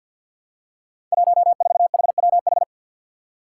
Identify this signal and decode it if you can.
Morse code, heard as a keyed tone, about 52 wpm, 700 Hz. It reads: 14HWH
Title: Five Challenging Letter-Number Combinations 50wpm